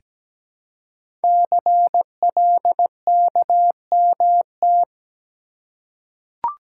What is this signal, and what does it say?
Morse code, heard as a keyed tone, about 17 wpm, 700 Hz. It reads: CLKMT
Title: Code Groups - 5 Letters - 17wpm